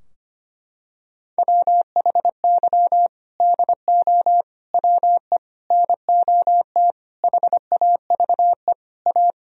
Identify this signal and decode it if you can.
Morse code, heard as a keyed tone, about 25 wpm, 700 Hz. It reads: WHY DO WE NOT HAVE A
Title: Sentences from Top 1000 Words - Review Part 3 - 25wpm